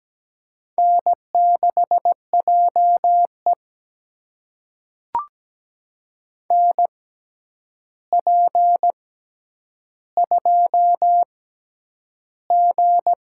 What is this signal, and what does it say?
Morse code, heard as a keyed tone, about 17 wpm, spaced as for 9 wpm, 700 Hz. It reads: N6JE NP2G
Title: USA Call Signs - 4 Characters - 6x Character Spacing - 17wpm